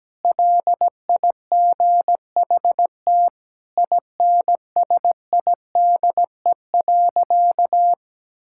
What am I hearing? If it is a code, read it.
Morse code, heard as a keyed tone, about 17 wpm, 700 Hz. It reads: LIGHT INSIDE.
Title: Sentences from Top 1000 Words - Review Part 6 - 17wpm